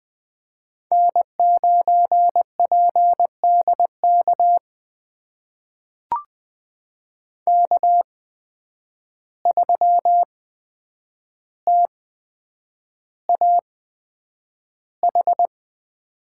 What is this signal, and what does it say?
Morse code, heard as a keyed tone, about 20 wpm, spaced as for 9 wpm, 700 Hz. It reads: N9PDK K3TAH
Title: USA Call Signs - 5 Characters - 8x Character Spacing - 20wpm